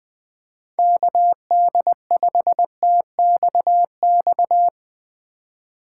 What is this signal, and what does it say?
Morse code, heard as a keyed tone, about 20 wpm, 700 Hz. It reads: KD5TXX
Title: K1USN SST QSO Practice - 20wpm